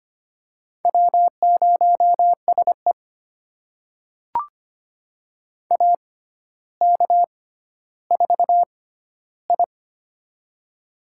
Morse code, heard as a keyed tone, about 25 wpm, spaced as for 13 wpm, 700 Hz. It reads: W0SE AK4I
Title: USA Call Signs - 4 Characters - 6x Character Spacing - 25wpm